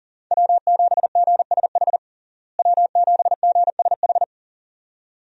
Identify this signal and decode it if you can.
Morse code, heard as a keyed tone, about 40 wpm, 700 Hz. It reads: W7GSH W7GSH
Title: USA Call Signs - 5 Characters - Repeated 2x - 40wpm